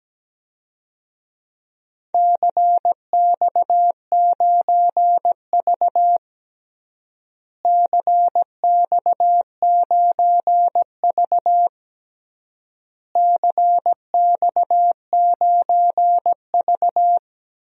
Morse code, heard as a keyed tone, about 17 wpm, 700 Hz. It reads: CX9V CX9V CX9V
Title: DXCC-100 Call Signs - 4 Characters - Repeated 3x - 17wpm